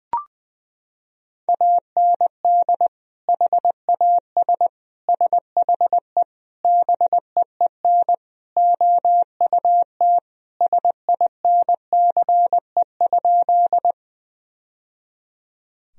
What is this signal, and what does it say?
Morse code, heard as a keyed tone, about 20 wpm, 700 Hz. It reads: AND HAS SHE BEEN OUT SINCE?
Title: Sentences from Top 1000 Words - Review Part 1 - 20wpm